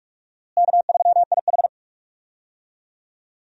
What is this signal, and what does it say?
Morse code, heard as a keyed tone, about 46 wpm, 700 Hz. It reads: K3IH
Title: USA Call Signs - 4 Characters - Repeated 3x - 45wpm